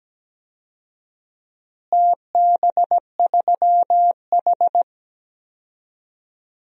Morse code, heard as a keyed tone, about 17 wpm, 700 Hz. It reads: TB3H
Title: DXCC-100 Call Signs - 4 Characters - Repeated 3x - 17wpm